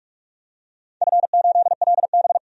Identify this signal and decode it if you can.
Morse code, heard as a keyed tone, about 46 wpm, 700 Hz. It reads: F8LB
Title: DXCC-100 Call Signs - 4 Characters - 45wpm